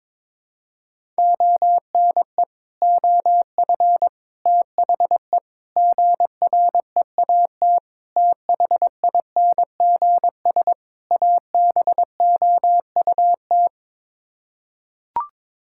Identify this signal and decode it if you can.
Morse code, heard as a keyed tone, about 22 wpm, 700 Hz. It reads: ONE OF THE GREAT THINGS ABOUT
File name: practice / 22wpm / Sets of 6 Words - 22wpm